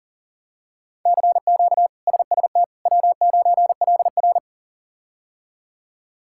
Morse code, heard as a keyed tone, about 40 wpm, 700 Hz. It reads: CQ SST W9LR
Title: K1USN SST QSO Practice - 40wpm